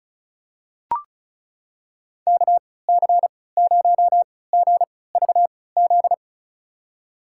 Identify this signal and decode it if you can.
Morse code, heard as a keyed tone, about 36 wpm, spaced as for 20 wpm, 700 Hz. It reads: KC0GVZ
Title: USA Call Signs - 3x Character Spacing - 35wpm